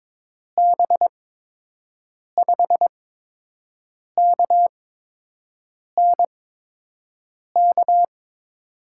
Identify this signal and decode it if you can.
Morse code, heard as a keyed tone, about 22 wpm, spaced as for 6 wpm, 700 Hz. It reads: B5KNK
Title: USA Call Signs - 8x Character Spacing - 22wpm